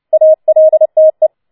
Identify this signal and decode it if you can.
Morse code, heard as a keyed tone, about 29 wpm, 630 Hz. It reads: ALTE
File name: Schneewittchen Teil3 30wpm